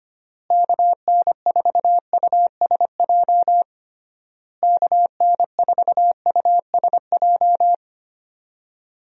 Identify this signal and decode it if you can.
Morse code, heard as a keyed tone, about 25 wpm, 700 Hz. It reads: KN4USJ KN4USJ
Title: USA Call Signs - 6 Characters - Repeated 3x - 25wpm